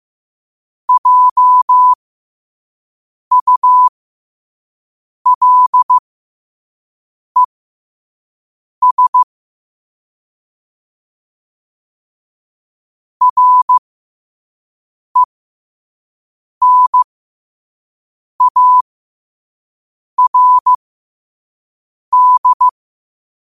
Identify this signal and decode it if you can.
Morse code, heard as a keyed tone, about 15 wpm, spaced as for 5 wpm, 1000 Hz. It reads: JULES RENARD
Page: New quotes every day in morse code at 5 Words per minute.